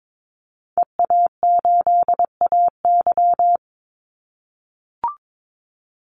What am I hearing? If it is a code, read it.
Morse code, heard as a keyed tone, about 22 wpm, 700 Hz. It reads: EA8AY
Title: Call Signs - 22wpm